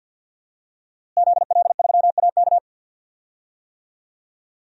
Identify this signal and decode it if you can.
Morse code, heard as a keyed tone, about 53 wpm, 700 Hz. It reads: ZL3AK
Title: DXCC-100 Call Signs - 5 Characters - Repeated 2x - 50wpm